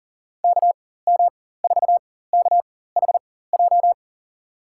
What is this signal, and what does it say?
Morse code, heard as a keyed tone, about 40 wpm, spaced as for 19 wpm, 700 Hz. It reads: KM4KHJ
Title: USA Call Signs - 6 Characters - 4x Character Spacing - 40wpm